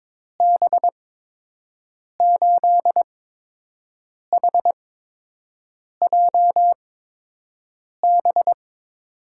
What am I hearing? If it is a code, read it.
Morse code, heard as a keyed tone, about 22 wpm, spaced as for 6 wpm, 700 Hz. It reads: B8HJB
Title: USA Call Signs - 6 Characters - 8x Character Spacing - 22wpm